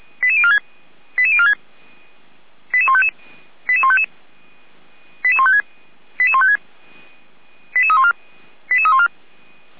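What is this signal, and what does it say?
Five-tone selective call (ZVEI), heard as quick runs of five tones, tones of about 70 ms. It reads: 88046 88046 88155 88155 88156 88156 88314 88314
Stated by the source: ZVEI five-tone call 88417 received on an SDR, recorded by feuerwehr-orlamuende (fire brigade Orlamünde).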